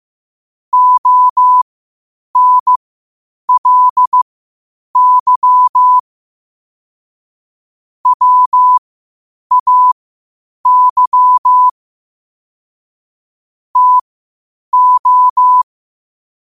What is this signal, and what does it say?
Morse code, heard as a keyed tone, about 15 wpm, spaced as for 8 wpm, 1000 Hz. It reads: ONLY WAY TO
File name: qotdfiles / 8wpm